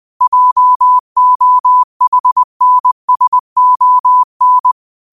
Morse code, heard as a keyed tone, about 20 wpm, 1000 Hz. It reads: JOHNSON
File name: qotdfiles / 20wpm